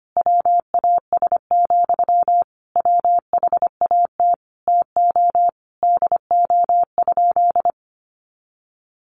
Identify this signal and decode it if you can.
Morse code, heard as a keyed tone, about 25 wpm, 700 Hz. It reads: WAS, WHAT TO DO?
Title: Sentences from Top 400 Words 25wpm